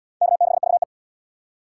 Morse code, heard as a keyed tone, about 77 wpm, 700 Hz. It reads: K6VE